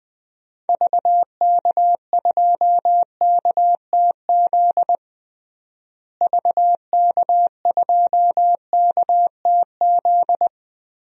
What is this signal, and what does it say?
Morse code, heard as a keyed tone, about 20 wpm, 700 Hz. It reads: VK2KTZ VK2KTZ
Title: DXCC-100 Call Signs - 6 Characters - Repeated 2x - 20wpm